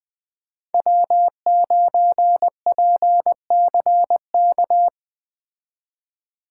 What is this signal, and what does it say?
Morse code, heard as a keyed tone, about 20 wpm, 700 Hz. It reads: W9PCK